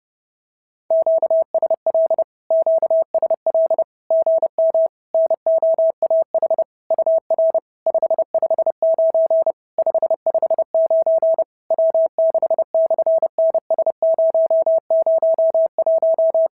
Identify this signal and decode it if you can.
Morse code, heard as a keyed tone, about 29 wpm, 650 Hz. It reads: QSL QSL GM NOAH UR 559 559 W6/NS001
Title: SOTA QSO Practice 30wpm